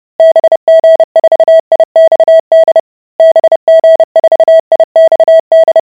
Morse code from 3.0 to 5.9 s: BG4IXD